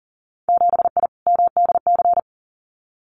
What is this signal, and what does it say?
Morse code, heard as a keyed tone, about 40 wpm, 700 Hz. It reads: 7I MDC
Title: Winter Field Day Exchanges Encore 40wpm